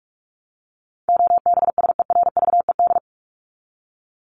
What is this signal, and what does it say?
Morse code, heard as a keyed tone, about 44 wpm, 700 Hz. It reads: OBSERVED
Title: ED Suffix - Top 100 Words with Base Words 45wpm